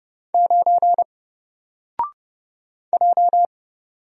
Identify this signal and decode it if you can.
Morse code, heard as a keyed tone, about 30 wpm, 700 Hz. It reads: 9 J
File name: Single-Letter-Number-30wpm